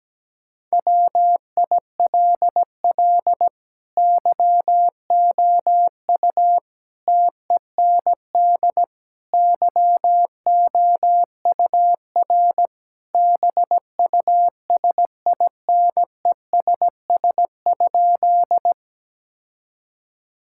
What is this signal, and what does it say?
Morse code, heard as a keyed tone, about 17 wpm, 700 Hz. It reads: WILL YOU TEND YOUR BUSINESS?
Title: Sentences from Top 1000 Words - Review Part 5 - 17wpm